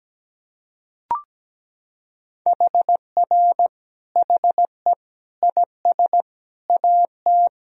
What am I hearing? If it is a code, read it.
Morse code, heard as a keyed tone, about 17 wpm, 700 Hz. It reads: HR HE IS AT
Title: Sentences from Top 300 Words and Abbreviations 17wpm